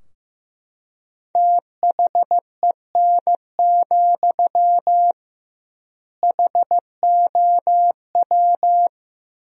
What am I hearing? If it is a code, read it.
Morse code, heard as a keyed tone, about 16 wpm, 700 Hz. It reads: THEN, HOW
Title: 2x Word-Spacing - Sentences from Top 100 Words - 15wpm